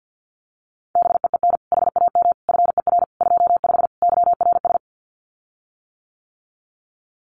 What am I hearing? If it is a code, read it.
Morse code, heard as a keyed tone, about 50 wpm, 700 Hz. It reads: BEEN HAM FER 25 YRS
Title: Common QSO Phrases Encore 50wpm